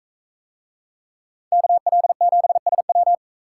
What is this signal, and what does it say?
Morse code, heard as a keyed tone, about 43 wpm, 700 Hz. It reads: KL7SW